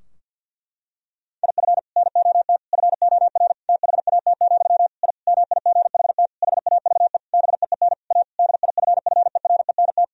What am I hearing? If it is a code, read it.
Morse code, heard as a keyed tone, about 50 wpm, 700 Hz. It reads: IF NOT FOR THAT, I MIGHT HAVE BEEN A DIFFERENT
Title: Sentences from Top 1000 Words - Review Part 3 - 50wpm